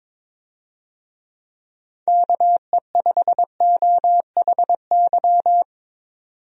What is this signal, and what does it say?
Morse code, heard as a keyed tone, about 22 wpm, 700 Hz. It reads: KE5OHY
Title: USA Call Signs - 6 Characters - 22wpm